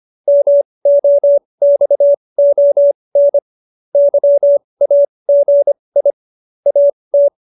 Morse code, heard as a keyed tone, about 25 wpm, spaced as for 20 wpm, 570 Hz. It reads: MOXON YAGI AT
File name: QSO310 20